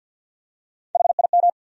Morse code, heard as a keyed tone, about 54 wpm, 700 Hz. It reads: HIM
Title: Top 100 Words 50wpm